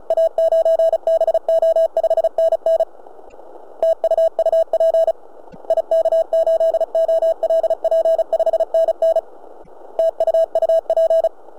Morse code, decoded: A9BO 5NN TU UP IK8OLP 5NN TU UP